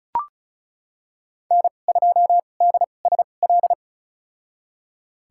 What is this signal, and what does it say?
Morse code, heard as a keyed tone, about 34 wpm, spaced as for 25 wpm, 700 Hz. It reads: N2DSL
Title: USA Call Signs - 5 Characters - 2x Character Spacing - 35wpm